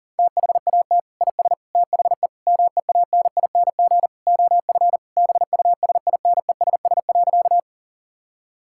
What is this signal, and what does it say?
Morse code, heard as a keyed tone, about 40 wpm, 700 Hz. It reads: THAT IS THE MEANING OF BUSINESS.
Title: Sentences from Top 1000 Words - Review Part 6 - 40wpm